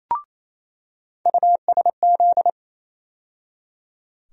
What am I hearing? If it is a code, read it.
Morse code, heard as a keyed tone, about 28 wpm, 700 Hz. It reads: USZ
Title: USA Call Sign Postfixes - 3 Letters 28wpm